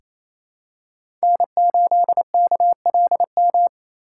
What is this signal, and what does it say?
Morse code, heard as a keyed tone, about 28 wpm, 700 Hz. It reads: N8KLM
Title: USA Call Signs - 6x Character Spacing - 28wpm